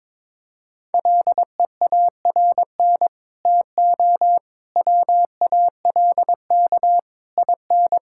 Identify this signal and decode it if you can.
Morse code, heard as a keyed tone, about 21 wpm, 700 Hz. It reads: LEARN TO WALK IN
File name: Sets of 9 Words - 22wpm